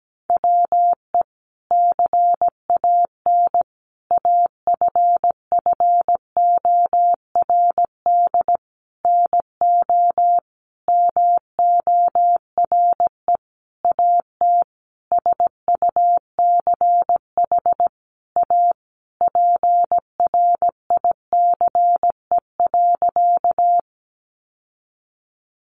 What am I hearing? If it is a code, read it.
Morse code, heard as a keyed tone, about 17 wpm, 700 Hz. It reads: WE CAN AFFORD NO MORE AT SUCH A PRICE.
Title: Sentences from Top 1100 Words 17wpm